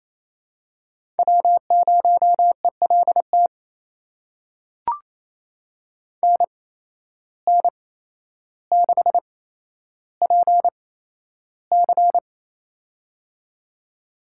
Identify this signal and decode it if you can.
Morse code, heard as a keyed tone, about 28 wpm, spaced as for 12 wpm, 700 Hz. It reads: W0ELT NN6PC
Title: USA Call Signs - 8x Character Spacing - 28wpm